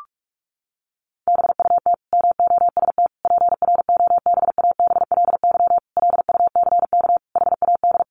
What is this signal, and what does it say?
Morse code, heard as a keyed tone, about 44 wpm, 700 Hz. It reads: BUT MOST PROBABLY LUCK HAD
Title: Sentences from Top 1500 Words 45wpm